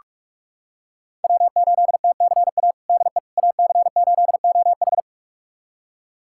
Morse code, heard as a keyed tone, about 42 wpm, 700 Hz. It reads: W8TCA DE AC8OH
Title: USA Callsign DE Callsign - 5 Characters - 45wpm